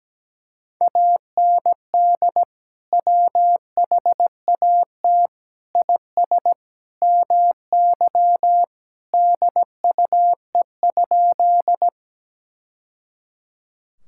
Morse code, heard as a keyed tone, about 17 wpm, 700 Hz. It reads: AND WHAT IS MY DUE?
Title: Sentences from Top 1000 Words - Review Part 6 - 17wpm